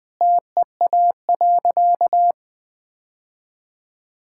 Morse code, heard as a keyed tone, about 20 wpm, 700 Hz. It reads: TEA.
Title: Sentences with 2-Word Phrases - 20wpm